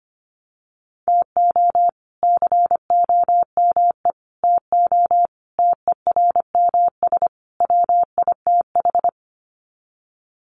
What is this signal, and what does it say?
Morse code, heard as a keyed tone, about 25 wpm, 700 Hz. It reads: TO COME TO TERMS WITH
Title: Sets of 5 Words Encore 25wpm